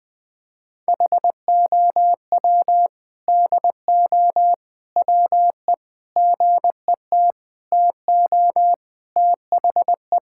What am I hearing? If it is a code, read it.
Morse code, heard as a keyed tone, about 20 wpm, 700 Hz. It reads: HOW DO WE GET TO THE
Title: Sentences with 4-Word Phrases - 20wpm